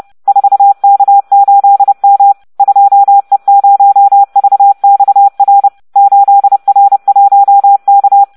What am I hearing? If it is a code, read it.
Morse code, heard as a keyed tone, about 30 wpm, 800 Hz. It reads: 4K8M 2E0VXR 8R1K